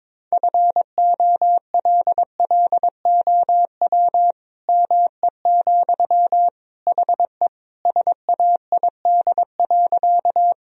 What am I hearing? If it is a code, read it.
Morse code, heard as a keyed tone, about 22 wpm, 700 Hz. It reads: FOLLOW ME, HE SAID.
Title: Sentences from Top 1000 Words - Review Part 1 - 22wpm